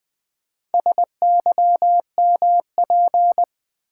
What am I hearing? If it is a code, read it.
Morse code, heard as a keyed tone, about 20 wpm, 700 Hz. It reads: SYMP
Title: Build A Word - 4 Letters At A Time - 11-Letter Words - 20wpm